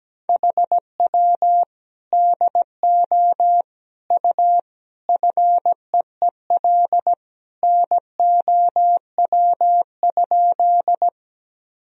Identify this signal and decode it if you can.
Morse code, heard as a keyed tone, about 17 wpm, 700 Hz. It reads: HW DO U FEEL NOW?
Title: Sentences from Top 400 Words and Abbreviations 17wpm